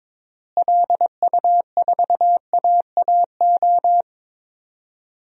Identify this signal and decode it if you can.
Morse code, heard as a keyed tone, about 22 wpm, 700 Hz. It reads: LU4AAO
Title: DXCC-100 Call Signs - 6 Characters - Repeated 3x - 22wpm